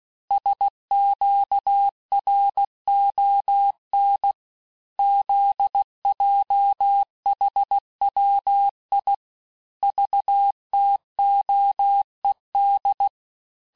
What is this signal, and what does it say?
Morse code, heard as a keyed tone, about 16 wpm, 800 Hz. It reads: SQRON ZJHWI VTOED
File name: cours-du-lundi files / Cours-1586-texte-chiffre-vitesse-15mpm